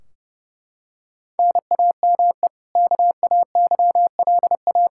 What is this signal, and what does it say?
Morse code, heard as a keyed tone, about 30 wpm, 700 Hz. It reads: NAME KAYLA
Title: Practice QSO - Exchange 1 - Beginning - 30wpm